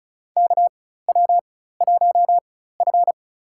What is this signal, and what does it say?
Morse code, heard as a keyed tone, about 36 wpm, spaced as for 16 wpm, 700 Hz. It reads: KW1F
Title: USA Call Signs - 4 Characters - 4x Character Spacing - 35wpm